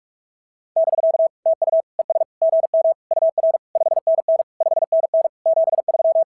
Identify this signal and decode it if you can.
Morse code, heard as a keyed tone, about 42 wpm, 660 Hz. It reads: <BK> TU ES GM UR 5NN 5NN 73
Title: SOTA QSO Practice 45wpm